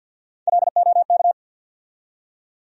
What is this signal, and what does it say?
Morse code, heard as a keyed tone, about 52 wpm, 700 Hz. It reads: LOK
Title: USA Call Sign Postfixes - 3 Letters 50wpm